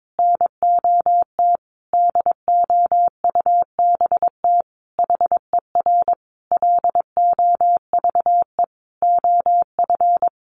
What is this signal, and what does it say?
Morse code, heard as a keyed tone, about 22 wpm, 700 Hz. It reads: NOT DOUBT HER LOVE OF
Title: Sentences from Top 1500 Words 22wpm